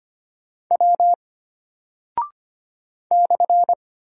Morse code, heard as a keyed tone, about 25 wpm, 700 Hz. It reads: W /
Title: Single Character (Letter:Number:Punctuation:Prosign) Alt 25wpm